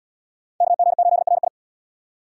Morse code, heard as a keyed tone, about 69 wpm, 700 Hz. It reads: KG8RI